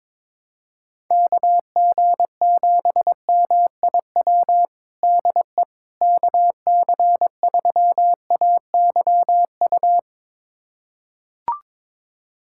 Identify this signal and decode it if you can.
Morse code, heard as a keyed tone, about 21 wpm, 700 Hz. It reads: KG7MIW DE KC3AYU
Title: USA Callsign DE Callsign 22wpm